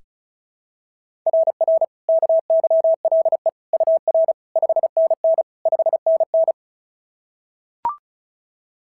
Morse code, heard as a keyed tone, about 36 wpm, 660 Hz. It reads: RR KYLE UR 5NN 5NN
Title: SOTA QSO Practice 35wpm